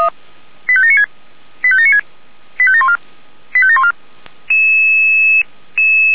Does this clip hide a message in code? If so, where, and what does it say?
Yes: five-tone selective call (ZVEI) from 0.7 to 3.9 s, heard as quick runs of five tones, tones of about 70 ms. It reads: 85796 85796 85714 85714